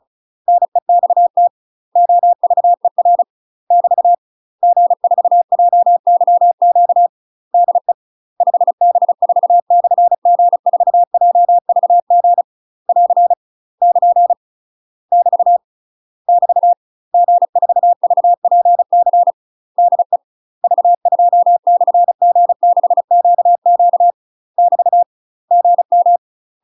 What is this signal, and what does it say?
Morse code, heard as a keyed tone, about 34 wpm, 700 Hz. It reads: NEXT OVER <BT> G4JYQ DE 5B4/G4JVG <AR> <KN> <BT> <BT> G4VPC DE V2/G6QQ <BT> GM
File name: UK MASTER QSOs TRAINING / 35WPM / UK MASTER QSOs 0021